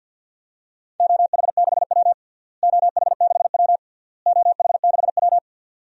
Morse code, heard as a keyed tone, about 49 wpm, 700 Hz. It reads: OH6W OH6W OH6W
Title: DXCC-100 Call Signs - 4 Characters - Repeated 3x - 50wpm